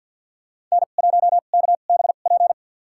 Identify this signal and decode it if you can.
Morse code, heard as a keyed tone, about 49 wpm, spaced as for 36 wpm, 690 Hz. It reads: N1KBP
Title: USA Call Signs - 2x Character Spacing - 50wpm